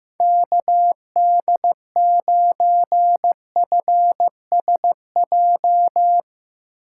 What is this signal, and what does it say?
Morse code, heard as a keyed tone, about 15 wpm, 700 Hz. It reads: KD9FSJ
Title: Speed Racing - USA 2x3 Call Signs 15wpm